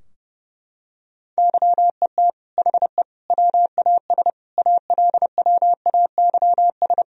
Morse code, heard as a keyed tone, about 32 wpm, 700 Hz. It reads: YET HE WAS ALWAYS
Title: Sentences from Top 1000 Words - Review Part 3 - 30wpm